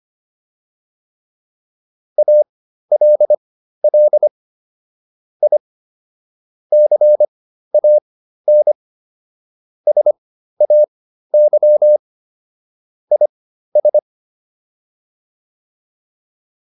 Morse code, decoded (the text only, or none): ALL I CAN SAY IS